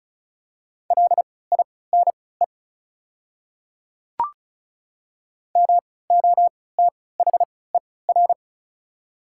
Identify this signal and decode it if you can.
Morse code, heard as a keyed tone, about 36 wpm, spaced as for 20 wpm, 700 Hz. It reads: LINE MOTHER
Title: 3x Character Spacing - Top 200 Words - 35wpm